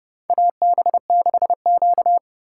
Morse code, heard as a keyed tone, about 29 wpm, 700 Hz. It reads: AB6Q
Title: USA Call Signs - 4x Character Spacing - 30wpm